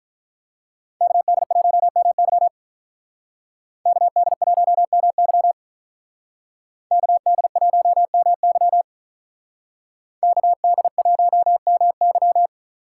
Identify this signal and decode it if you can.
Morse code, heard as a keyed tone, about 43 wpm, 700 Hz. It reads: KD1MY KD1MY KD1MY KD1MY
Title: Speed Racing - USA 2x2 Call Signs 35wpm